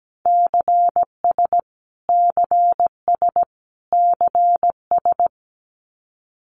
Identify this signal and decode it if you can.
Morse code, heard as a keyed tone, about 17 wpm, 700 Hz. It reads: CS CS CS
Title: Abbreviations Repeated 3x - 17wpm